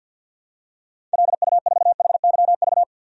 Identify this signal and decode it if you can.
Morse code, heard as a keyed tone, about 53 wpm, 700 Hz. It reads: LU4HYV
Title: DXCC-100 Call Signs - 6 Characters - 50wpm